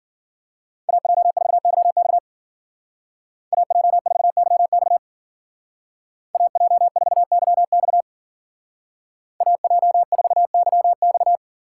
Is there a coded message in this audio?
Yes: Morse code from 0.9 to 11.4 s, heard as a keyed tone, about 52 wpm, 700 Hz. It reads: AJ4YX AJ4YX AJ4YX AJ4YX